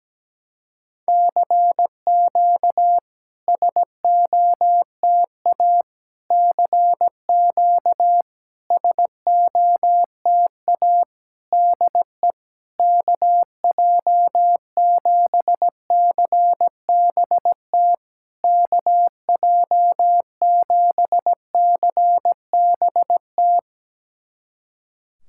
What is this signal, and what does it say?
Morse code, heard as a keyed tone, about 17 wpm, 700 Hz. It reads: CQ SOTA CQ SOTA DE KJ7CBT KJ7CBT
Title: SOTA QSO Practice 17wpm